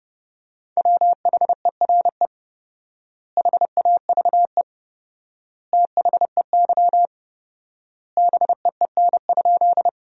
Morse code, heard as a keyed tone, about 30 wpm, 700 Hz. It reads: WHERE HAVE THEY BEEN?
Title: 4x Word-Spacing - Sentences from Top 200 Words - 30wpm